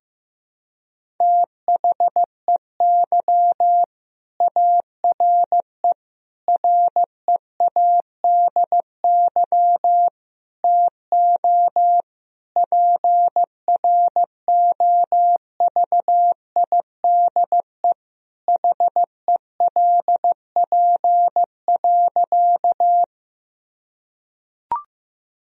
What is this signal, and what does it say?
Morse code, heard as a keyed tone, about 15 wpm, 700 Hz. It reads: THEY ARE READY TO PROVIDE HELP.
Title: Sentences from Top 1000 Words - Review Part 3 - 15wpm